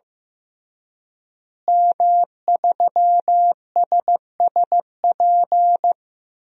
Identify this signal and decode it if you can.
Morse code, heard as a keyed tone, about 15 wpm, 700 Hz. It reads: M3SSP
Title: DXCC-100 Call Signs - 5 Characters - Repeated 3x - 15wpm